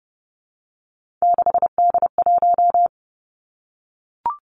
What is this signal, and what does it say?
Morse code, heard as a keyed tone, about 32 wpm, 700 Hz. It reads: 6D1
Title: Three Challenging Letter-Number Combinations 30wpm